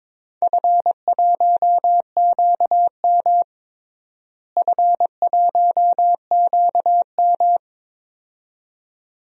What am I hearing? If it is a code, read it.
Morse code, heard as a keyed tone, about 22 wpm, 700 Hz. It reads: F1QM F1QM
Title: DXCC-100 Call Signs - 4 Characters - Repeated 2x - 22wpm